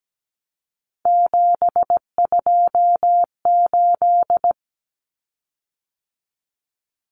Morse code, heard as a keyed tone, about 17 wpm, 700 Hz. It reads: 728
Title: Common Cut Numbers - 3-Digit 17wpm